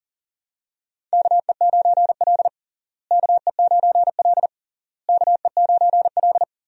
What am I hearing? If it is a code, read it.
Morse code, heard as a keyed tone, about 40 wpm, 700 Hz. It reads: KE9L KE9L KE9L
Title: USA Call Signs - 4 Characters - Repeated 3x - 40wpm